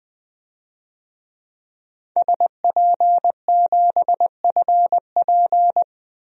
Morse code, heard as a keyed tone, about 20 wpm, 700 Hz. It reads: SP7FP
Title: DXCC-100 Call Signs - 5 Characters - 20wpm